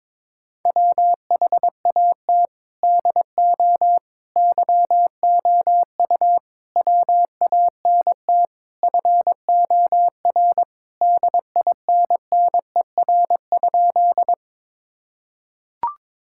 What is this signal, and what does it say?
Morse code, heard as a keyed tone, about 22 wpm, 700 Hz. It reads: WHAT DO YOU WANT FOR DINNER?
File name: practice / 22wpm / Sentences with 4-Word Phrases - 22wpm